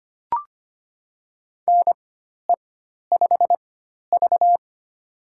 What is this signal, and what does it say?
Morse code, heard as a keyed tone, about 25 wpm, spaced as for 12 wpm, 700 Hz. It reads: NE5V
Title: USA Call Signs - 4 Characters - 4x Character Spacing - 25wpm